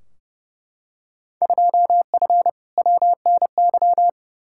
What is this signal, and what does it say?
Morse code, heard as a keyed tone, about 30 wpm, 700 Hz. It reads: 2F WNY
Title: Field Day Exchanges Encore - 30wpm